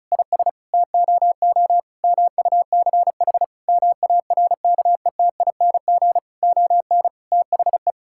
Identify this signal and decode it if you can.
Morse code, heard as a keyed tone, about 36 wpm, 700 Hz. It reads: IS TOO MUCH MARKETING ON THE
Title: Sentences from Top 1000 Words - Review Part 5 - 35wpm